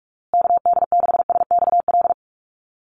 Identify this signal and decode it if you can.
Morse code, heard as a keyed tone, about 44 wpm, 700 Hz. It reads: KD6SXL
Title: USA Call Signs Repeated 2x 45wpm